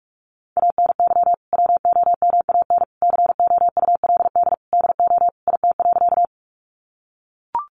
Morse code, heard as a keyed tone, about 44 wpm, 700 Hz. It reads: ANY WOMAN COULD DO IT.
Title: Sentences from Top 1100 Words 45wpm